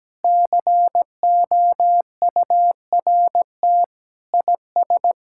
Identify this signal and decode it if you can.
Morse code, heard as a keyed tone, about 17 wpm, 700 Hz. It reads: COURT IS
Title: Sentences from Top 1600 Words 17wpm